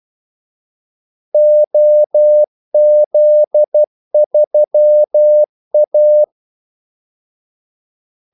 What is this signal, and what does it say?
Morse code, heard as a keyed tone, about 12 wpm, 600 Hz. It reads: OZ3A
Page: 2 ekstra ordmellemrum mellem kaldesignalerne